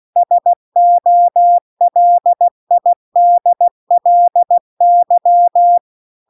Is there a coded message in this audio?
Yes: Morse code from 0.2 to 5.8 s, heard as a keyed tone, about 16 wpm, 700 Hz. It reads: SOLIDLY